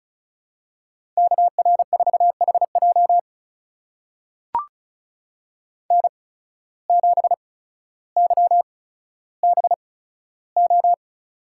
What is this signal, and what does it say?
Morse code, heard as a keyed tone, about 36 wpm, spaced as for 15 wpm, 700 Hz. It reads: KR4HJ N7YBO
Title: USA Call Signs - 5 Characters - 8x Character Spacing - 35wpm